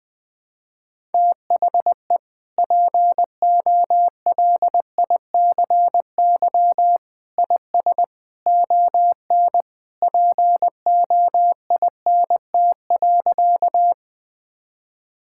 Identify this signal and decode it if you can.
Morse code, heard as a keyed tone, about 20 wpm, 700 Hz. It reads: THE POLICY IS ON POINT.
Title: Sentences from Top 1000 Words - Review Part 5 - 20wpm